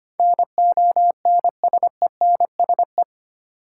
Morse code, heard as a keyed tone, about 25 wpm, 690 Hz. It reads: NONSENSE
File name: 8 Letter Words - First 3 Letters Given - 25wpm